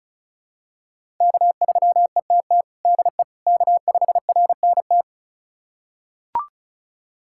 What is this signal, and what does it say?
Morse code, heard as a keyed tone, about 34 wpm, 700 Hz. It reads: K3ETT DE K5RNT
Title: USA Callsign DE Callsign - 5 Characters - 35wpm